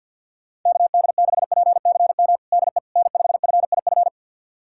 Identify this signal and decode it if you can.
Morse code, heard as a keyed tone, about 53 wpm, 700 Hz. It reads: KD6PCM DE N5FIF
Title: USA Callsign DE Callsign 50wpm